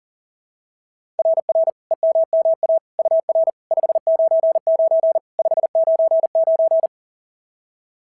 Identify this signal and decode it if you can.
Morse code, heard as a keyed tone, about 40 wpm, 650 Hz. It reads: RR EMMA UR 599 599
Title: SOTA QSO Practice 40wpm